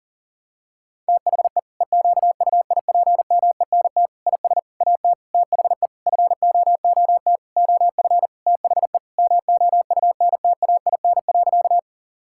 Morse code, heard as a keyed tone, about 40 wpm, 700 Hz. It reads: THE EQUIPMENT IS AT THE FOOT OF THE MOUNTAIN.
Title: Sentences from Top 1000 Words - Review Part 5 - 40wpm